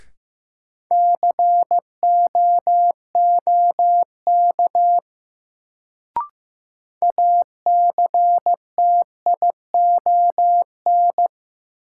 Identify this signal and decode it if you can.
Morse code, heard as a keyed tone, about 15 wpm, 700 Hz. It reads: COOK ACTION
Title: Sentences from Top 700 Words - New Word Drill - 15wpm